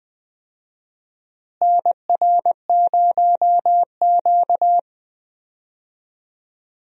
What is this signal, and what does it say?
Morse code, heard as a keyed tone, about 20 wpm, 700 Hz. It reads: NR0Q
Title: USA Call Signs - 4 Characters - Repeated 2x - 20wpm